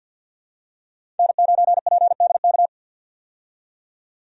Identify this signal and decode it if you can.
Morse code, heard as a keyed tone, about 52 wpm, 700 Hz. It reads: N9PDK
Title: USA Call Signs - 5 Characters - 50wpm